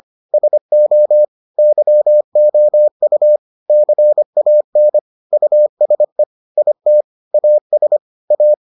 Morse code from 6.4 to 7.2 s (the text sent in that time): IT